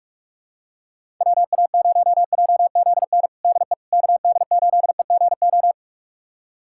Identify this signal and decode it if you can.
Morse code, heard as a keyed tone, about 46 wpm, 700 Hz. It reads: WA0JZN DE KD8EGO
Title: USA Callsign DE Callsign 45wpm